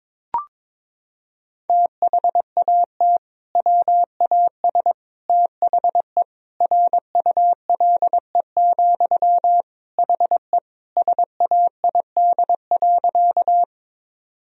Morse code, heard as a keyed tone, about 21 wpm, 700 Hz. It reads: THAT WAS THE RULE, HE SAID.
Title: Sentences from Top 1000 Words - Review Part 2 - 22wpm